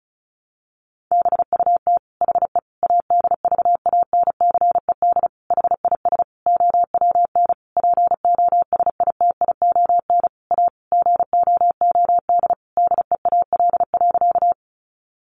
Morse code, heard as a keyed tone, about 34 wpm, 700 Hz. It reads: BUT HE ADVANCED HIS OWN POSITION A GOOD DEAL.
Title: Sentences from Top 1200 Words 35wpm